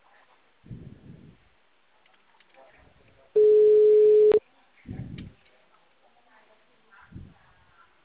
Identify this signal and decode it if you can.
Ringback tone (Europe, 425 Hz), with speech.